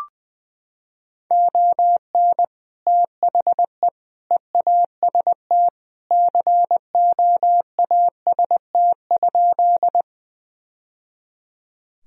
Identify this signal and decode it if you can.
Morse code, heard as a keyed tone, about 20 wpm, 700 Hz. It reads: ON THE EAST COAST?
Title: Sentences from Top 1000 Words - Review Part 4 - 20wpm